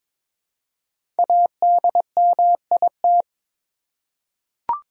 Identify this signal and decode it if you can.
Morse code, heard as a keyed tone, about 22 wpm, 690 Hz. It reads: ADMIT
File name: 5 Letter Words - First 3 Letters Given - 22wpm